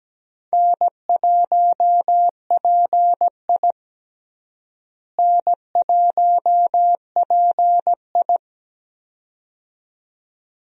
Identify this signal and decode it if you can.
Morse code, heard as a keyed tone, about 17 wpm, 700 Hz. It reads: N1PI N1PI
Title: USA Call Signs - 4 Characters - Repeated 2x - 17wpm